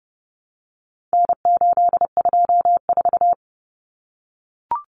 Morse code, heard as a keyed tone, about 30 wpm, 700 Hz. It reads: N824
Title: Common Cut Numbers - 4-Digit 30wpm